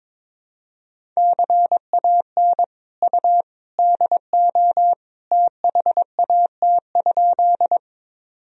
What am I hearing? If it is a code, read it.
Morse code, heard as a keyed tone, about 21 wpm, 700 Hz. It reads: CAN U DO THAT?
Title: Sentences from Top 100 Words and Abbreviations 22wpm